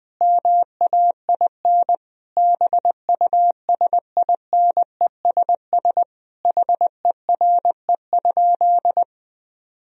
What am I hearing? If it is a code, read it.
Morse code, heard as a keyed tone, about 21 wpm, 700 Hz. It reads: MAIN BUSINESS HERE?
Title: Sentences from Top 1000 Words - Review Part 2 - 20wpm